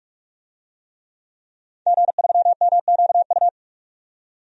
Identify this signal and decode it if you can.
Morse code, heard as a keyed tone, about 46 wpm, 700 Hz. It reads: G3MQU
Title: DXCC-100 Call Signs - 5 Characters - 45wpm